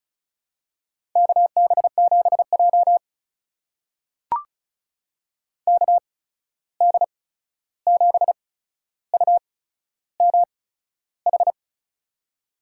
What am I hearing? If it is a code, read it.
Morse code, heard as a keyed tone, about 36 wpm, spaced as for 13 wpm, 700 Hz. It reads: KB7J KD7UMH
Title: USA Call Signs - 8x Character Spacing - 35wpm